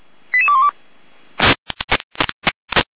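Five-tone selective call (ZVEI), heard as quick runs of five tones, tones of about 70 ms. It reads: 88212
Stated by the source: ZVEI selective call received on SDR.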